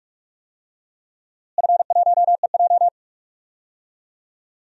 Morse code, heard as a keyed tone, about 46 wpm, 700 Hz. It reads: F1EJ
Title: DXCC-100 Call Signs - 4 Characters - Repeated 2x - 45wpm